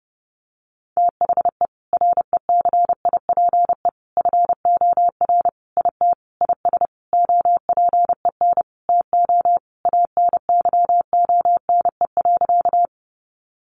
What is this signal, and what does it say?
Morse code, heard as a keyed tone, about 30 wpm, 700 Hz. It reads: THE RECIPE FOR IT IS OPEN TO ANYONE.
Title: Sentences from Top 1200 Words 30wpm